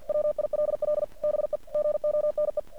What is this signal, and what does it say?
Morse code, heard as a keyed tone, about 50 wpm, 600 Hz. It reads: WILL BE GONE
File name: archived / CRSnet